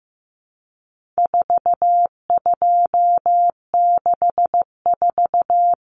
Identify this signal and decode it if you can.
Morse code, heard as a keyed tone, about 15 wpm, 700 Hz. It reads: 4264
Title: Numbers - 4-Digit 15wpm